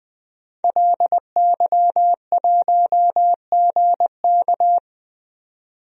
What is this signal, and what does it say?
Morse code, heard as a keyed tone, about 20 wpm, 700 Hz. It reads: LY1GK
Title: DXCC-100 Call Signs - 5 Characters - 20wpm